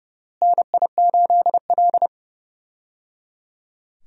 Morse code, heard as a keyed tone, about 29 wpm, 700 Hz. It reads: NI8L
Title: USA Call Signs - 4 Characters - Repeated 3x - 30wpm